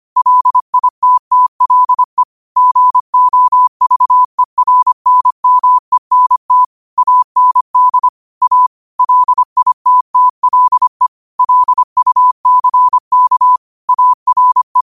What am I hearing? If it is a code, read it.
Morse code, heard as a keyed tone, about 25 wpm, 1000 Hz. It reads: LITTLE GOVERNMENT AND A LITTLE LUCK ARE
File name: qotdfiles / 25wpm